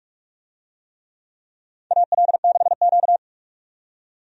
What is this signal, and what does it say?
Morse code, heard as a keyed tone, about 46 wpm, 700 Hz. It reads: AL6Q
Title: USA Call Signs - 4 Characters - 45wpm